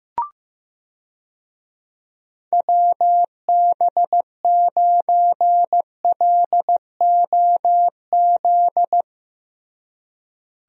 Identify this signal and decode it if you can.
Morse code, heard as a keyed tone, about 16 wpm, 700 Hz. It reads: WB9LOZ
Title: USA Call Signs - 6 Characters - Repeated 2x - 15wpm